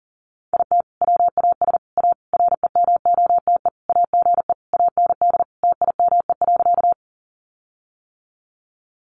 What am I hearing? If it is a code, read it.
Morse code, heard as a keyed tone, about 40 wpm, 700 Hz. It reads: IT WAS A REMOTE AGE AND TIME.
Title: Sentences from Top 1300 Words 40wpm